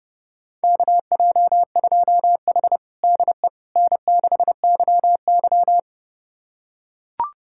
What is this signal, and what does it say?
Morse code, heard as a keyed tone, about 32 wpm, 700 Hz. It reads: KJ2H DE N6YY
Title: USA Callsign DE Callsign - 4 Characters - 30wpm